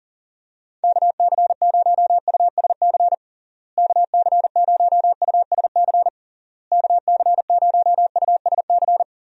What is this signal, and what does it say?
Morse code, heard as a keyed tone, about 40 wpm, 700 Hz. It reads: KC0USC KC0USC KC0USC
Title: USA Call Signs - 6 Characters - Repeated 3x - 40wpm